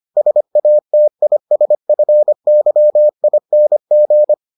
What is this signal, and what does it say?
Morse code, heard as a keyed tone, about 25 wpm, 600 Hz. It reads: SATISFYING